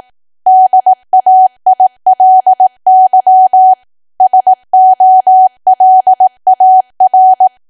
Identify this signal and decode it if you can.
Morse code, heard as a keyed tone, about 18 wpm, 750 Hz. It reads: DAILY SOLAR
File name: cwa / W1AW-Level2 / 18WPM